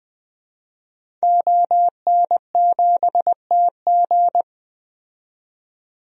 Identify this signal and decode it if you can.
Morse code, heard as a keyed tone, about 20 wpm, 700 Hz. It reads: ON7TG
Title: DXCC-100 Call Signs - 5 Characters - 20wpm